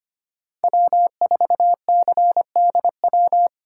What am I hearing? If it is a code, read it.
Morse code, heard as a keyed tone, about 25 wpm, 700 Hz. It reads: W4CDW
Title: USA Call Signs - 5 Characters - 4x Character Spacing - 25wpm